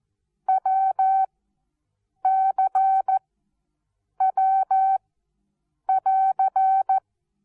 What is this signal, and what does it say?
Morse code, heard as a keyed tone, about 14 wpm, 770 Hz. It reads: W C W <AR>
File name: Onlinelehrgang / morsen / DJ4UF Morsen19